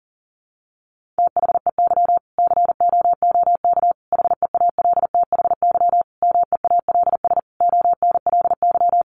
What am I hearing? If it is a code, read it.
Morse code, heard as a keyed tone, about 37 wpm, 700 Hz. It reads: THEY COOK HEALTHY MEALS ONLY
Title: Easy Phrases 40wpm